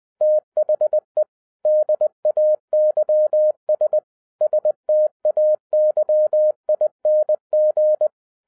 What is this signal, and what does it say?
Morse code, heard as a keyed tone, about 20 wpm, 610 Hz. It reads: THE DAYS STAYING
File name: 2011-07-mp3 / n48 JOHN K3TN 20wpm